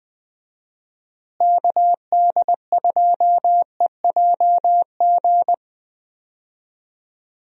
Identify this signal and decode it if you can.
Morse code, heard as a keyed tone, about 20 wpm, 700 Hz. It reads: KD2EJG
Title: USA Call Signs - 6 Characters - Repeated 3x - 20wpm